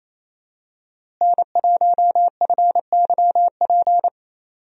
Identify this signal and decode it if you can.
Morse code, heard as a keyed tone, about 28 wpm, 700 Hz. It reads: N1FYP
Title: USA Call Signs - 5 Characters - Repeated 2x - 28wpm